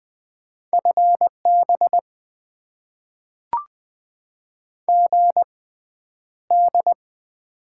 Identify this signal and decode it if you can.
Morse code, heard as a keyed tone, about 20 wpm, spaced as for 10 wpm, 700 Hz. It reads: FB GD
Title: Top 90 QSO Elements - 6x Character Spacing - 20wpm